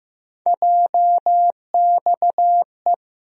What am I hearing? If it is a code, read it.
Morse code, heard as a keyed tone, about 16 wpm, 700 Hz. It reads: JXE
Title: USA Call Sign Postfixes - 3 Letters 15wpm